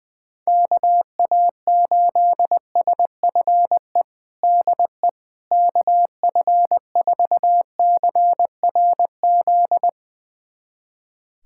Morse code, heard as a keyed tone, about 20 wpm, 700 Hz. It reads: KA8SFE DE KF4CRZ
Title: USA Callsign DE Callsign - 6 Characters - 20wpm